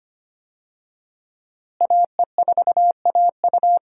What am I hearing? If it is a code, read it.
Morse code, heard as a keyed tone, about 25 wpm, 700 Hz. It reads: AE4AU
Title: USA Call Signs - 5 Characters - Repeated 2x - 25wpm